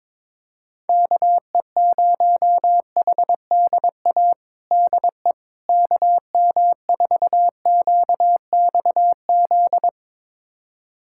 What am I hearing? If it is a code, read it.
Morse code, heard as a keyed tone, about 22 wpm, 700 Hz. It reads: KE0HDA DE KM4QXZ
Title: USA Callsign DE Callsign - 6 Characters - 22wpm